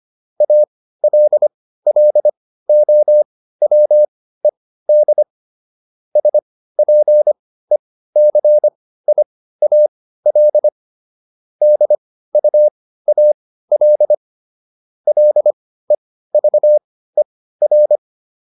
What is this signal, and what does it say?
Morse code, heard as a keyed tone, about 25 wpm, spaced as for 15 wpm, 600 Hz. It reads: ALLOWED SPECIAL DUAL LEVER
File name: SS215 15